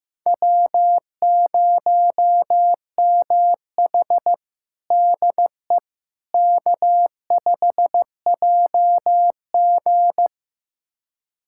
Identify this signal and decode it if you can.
Morse code, heard as a keyed tone, about 15 wpm, 700 Hz. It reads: W0MH DE K5JG
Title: USA Callsign DE Callsign - 4 Characters - 15wpm